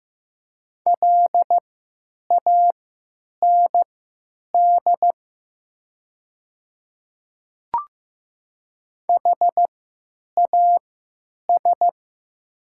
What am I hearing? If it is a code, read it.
Morse code, heard as a keyed tone, about 15 wpm, spaced as for 9 wpm, 700 Hz. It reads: LAND HAS
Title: 3x Character Spacing - Top 300 Words - 15wpm